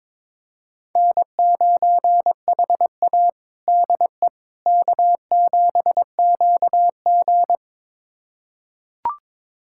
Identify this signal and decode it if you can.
Morse code, heard as a keyed tone, about 21 wpm, 700 Hz. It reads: N9HA DE K7QG
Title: USA Callsign DE Callsign - 4 Characters - 22wpm